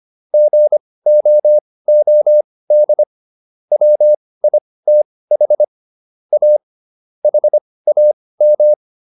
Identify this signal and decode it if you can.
Morse code, heard as a keyed tone, about 25 wpm, spaced as for 18 wpm, 600 Hz. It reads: GOOD WITH A HAM